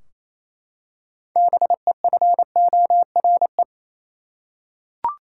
Morse code, heard as a keyed tone, about 28 wpm, 700 Hz. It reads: BEFORE
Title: Six Letter Words - 28wpm